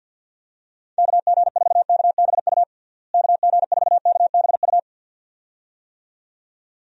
Morse code, heard as a keyed tone, about 52 wpm, 700 Hz. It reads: KG4KBU KG4KBU
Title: USA Call Signs - 6 Characters - Repeated 2x - 50wpm